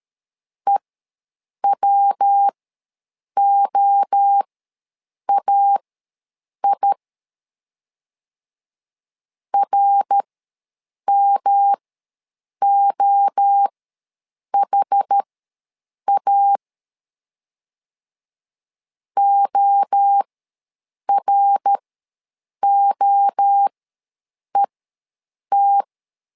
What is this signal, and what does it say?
Morse code, heard as a keyed tone, about 13 wpm, spaced as for 7 wpm, 800 Hz. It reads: EWOAI RMOHA OROET
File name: LEZIONI / 3 MWRO / 3 MWRO ws 12wpm 1914